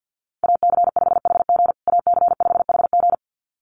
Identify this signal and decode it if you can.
Morse code, heard as a keyed tone, about 50 wpm, 700 Hz. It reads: AC5HG AC5HG
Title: USA Call Signs Repeated 2x 50wpm